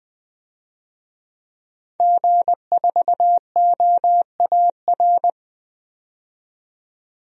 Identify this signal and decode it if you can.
Morse code, heard as a keyed tone, about 20 wpm, 700 Hz. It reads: G4OAR
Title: DXCC-100 Call Signs - 5 Characters - 20wpm